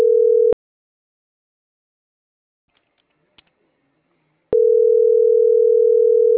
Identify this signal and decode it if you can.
Ringback tone (North America), 440 + 480 Hz.